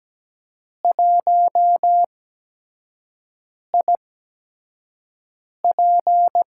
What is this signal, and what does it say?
Morse code, heard as a keyed tone, about 17 wpm, spaced as for 5 wpm, 700 Hz. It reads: 1IP
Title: USA Call Signs - 8x Character Spacing - 17wpm